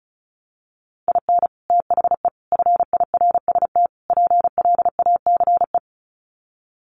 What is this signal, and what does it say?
Morse code, heard as a keyed tone, about 34 wpm, 700 Hz. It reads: IN THE FIRST PLACE
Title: Sets of 4 Words 35wpm